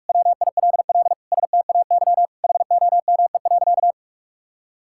Morse code, heard as a keyed tone, about 42 wpm, 700 Hz. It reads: WILL STAY HOME.
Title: Sentences from Top 1000 Words - Review Part 6 - 45wpm